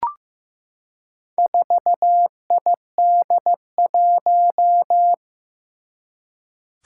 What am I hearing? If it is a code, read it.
Morse code, heard as a keyed tone, about 15 wpm, 700 Hz. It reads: 4ID1
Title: Four Challenging Letter-Number Combinations 15wpm